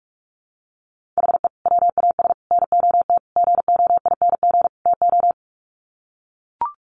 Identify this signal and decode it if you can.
Morse code, heard as a keyed tone, about 44 wpm, 700 Hz. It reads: HE WAS NOT GOING TO
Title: Sets of 5 Words 45wpm